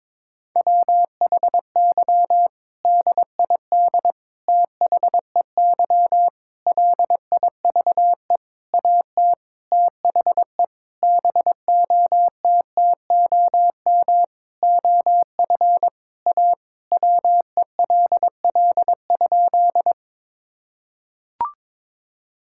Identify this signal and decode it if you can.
Morse code, heard as a keyed tone, about 22 wpm, 700 Hz. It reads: WHY DID THEY LIVE AT THE BOTTOM OF A WELL?
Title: Sentences from Top 1000 Words - Review Part 6 - 22wpm